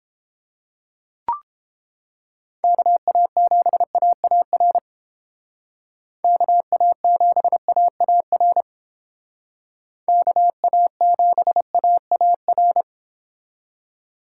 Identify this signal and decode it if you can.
Morse code, heard as a keyed tone, about 29 wpm, 700 Hz. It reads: KA7AAR KA7AAR KA7AAR